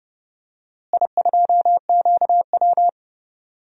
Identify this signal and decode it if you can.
Morse code, heard as a keyed tone, about 31 wpm, 700 Hz. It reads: I2QW
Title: DXCC-100 Call Signs - 4 Characters - Repeated 3x - 30wpm